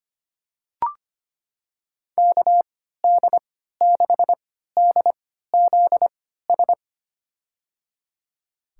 Morse code, heard as a keyed tone, about 25 wpm, spaced as for 14 wpm, 700 Hz. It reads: KD6DZS
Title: USA Call Signs - 6 Characters - 3x Character Spacing - 25wpm